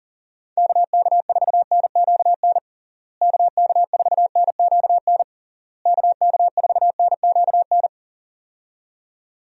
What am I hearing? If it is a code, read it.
Morse code, heard as a keyed tone, about 37 wpm, 700 Hz. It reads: KK4NQN KK4NQN KK4NQN
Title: USA Call Signs - 6 Characters - Repeated 3x - 40wpm